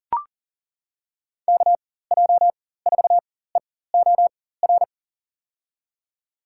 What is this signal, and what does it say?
Morse code, heard as a keyed tone, about 40 wpm, spaced as for 19 wpm, 700 Hz. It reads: KJ4EOR
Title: USA Call Signs - 6 Characters - 4x Character Spacing - 40wpm